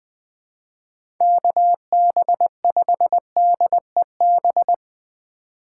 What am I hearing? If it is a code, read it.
Morse code, heard as a keyed tone, about 23 wpm, 700 Hz. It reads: KB5DEB
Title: Speed Racing - USA Call Signs 20wpm